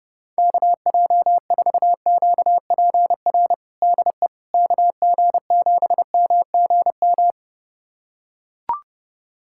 Morse code, heard as a keyed tone, about 32 wpm, 700 Hz. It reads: KJ4QPR DE KG7MGM
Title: USA Callsign DE Callsign 30wpm